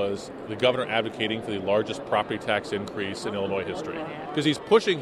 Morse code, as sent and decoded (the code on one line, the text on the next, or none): none